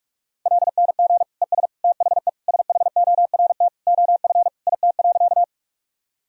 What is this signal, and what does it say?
Morse code, heard as a keyed tone, about 48 wpm, 700 Hz. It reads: LNG ES THE SHORT OF IT.
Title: Sentences from Top 400 Words and Abbreviations 45wpm